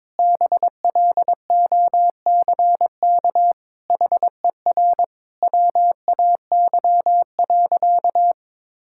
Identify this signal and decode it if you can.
Morse code, heard as a keyed tone, about 22 wpm, 700 Hz. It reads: BLOCK HER WAY.
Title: Sentences from Top 1600 Words 22wpm